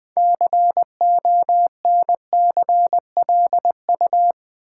Morse code, heard as a keyed tone, about 20 wpm, 700 Hz. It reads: CONCLU
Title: Build A Word - 3 Letters At A Time - 8-Letter Words - 20wpm